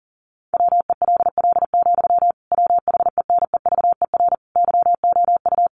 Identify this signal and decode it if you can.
Morse code, heard as a keyed tone, about 40 wpm, 700 Hz. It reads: WELL, WHENEVER YOU
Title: Sentences from Top 1200 Words 40wpm